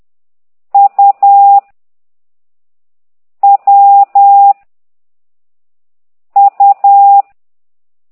Morse code, heard as a keyed tone, about 10 wpm, 800 Hz. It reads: U W U
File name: ct2idw / exerc8